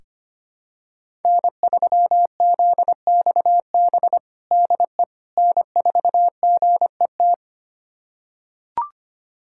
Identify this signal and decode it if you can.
Morse code, heard as a keyed tone, about 25 wpm, 700 Hz. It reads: N3ZXB DE N4GET
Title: USA Callsign DE Callsign - 5 Characters - 25wpm